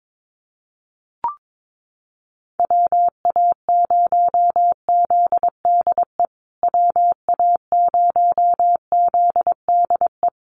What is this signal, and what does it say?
Morse code, heard as a keyed tone, about 22 wpm, 700 Hz. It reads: WA0ZDE WA0ZDE
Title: USA Call Signs Repeated 3x 22wpm